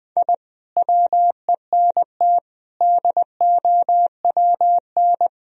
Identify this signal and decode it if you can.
Morse code, heard as a keyed tone, about 21 wpm, 700 Hz. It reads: I WENT DOWN
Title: Sentences from Top 1000 Words - Review Part 2 - 20wpm